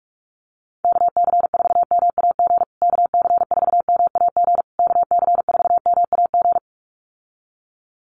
Morse code, heard as a keyed tone, about 44 wpm, 700 Hz. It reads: KC4MAG KC4MAG KC4MAG
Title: USA Call Signs Repeated 3x 45wpm